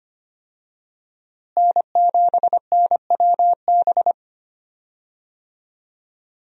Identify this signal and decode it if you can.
Morse code, heard as a keyed tone, about 25 wpm, 700 Hz. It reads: N7NWB
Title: USA Call Signs - 5 Characters - Repeated 2x - 25wpm